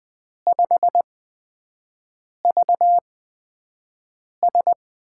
Morse code, heard as a keyed tone, about 20 wpm, spaced as for 5 wpm, 700 Hz. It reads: 5VS
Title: USA Call Signs - 5 Characters - 8x Character Spacing - 20wpm